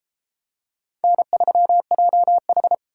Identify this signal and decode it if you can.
Morse code, heard as a keyed tone, about 29 wpm, 700 Hz. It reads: N3JH